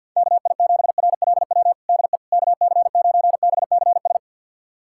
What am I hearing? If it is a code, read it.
Morse code, heard as a keyed tone, about 48 wpm, 690 Hz. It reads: KI7RLW DE KC9BCS
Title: USA Callsign DE Callsign - 6 Characters - 50wpm